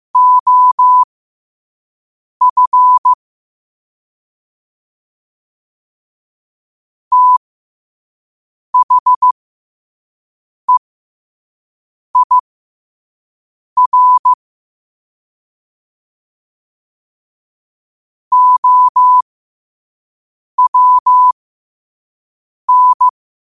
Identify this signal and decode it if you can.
Morse code, heard as a keyed tone, about 15 wpm, spaced as for 5 wpm, 1000 Hz. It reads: OF THEIR OWN
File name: qotdfiles / 5wpm